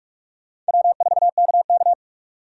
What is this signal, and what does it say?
Morse code, heard as a keyed tone, about 60 wpm, spaced as for 52 wpm, 700 Hz. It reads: W4KK
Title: Speed Racing - USA 1x2 Call Signs 45wpm